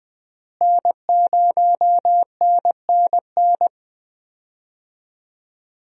Morse code, heard as a keyed tone, about 21 wpm, 700 Hz. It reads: N0NNN
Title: USA Call Signs - 5 Characters - Repeated 3x - 20wpm